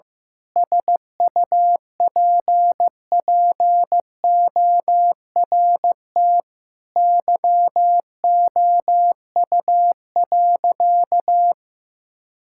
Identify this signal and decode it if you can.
Morse code, heard as a keyed tone, about 15 wpm, 700 Hz. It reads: SUPPORT YOU.
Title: Sentences from Top 1000 Words - Review Part 5 - 15wpm